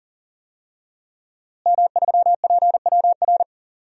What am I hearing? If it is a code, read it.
Morse code, heard as a keyed tone, about 40 wpm, 700 Hz. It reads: M3PWR
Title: DXCC-100 Call Signs - 5 Characters - Repeated 2x - 40wpm